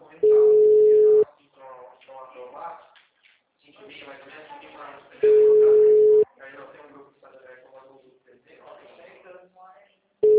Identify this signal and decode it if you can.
Ringback tone (Europe, 425 Hz), with speech.